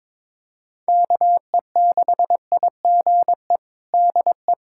Morse code, heard as a keyed tone, about 22 wpm, 700 Hz. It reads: KE6IGE DE
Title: USA Callsign DE Callsign - 6 Characters - 22wpm